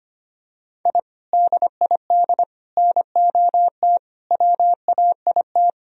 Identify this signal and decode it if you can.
Morse code, heard as a keyed tone, about 25 wpm, 700 Hz. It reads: I DID NOT WAIT
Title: Sentences from Top 1000 Words - Review Part 2 - 25wpm